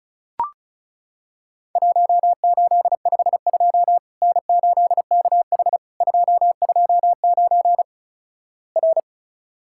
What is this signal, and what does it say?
Morse code, heard as a keyed tone, about 36 wpm, 700 Hz. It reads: 1852 N8KH 229 R
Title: Worked All Europe DX Contest - QTC Exchange - 35wpm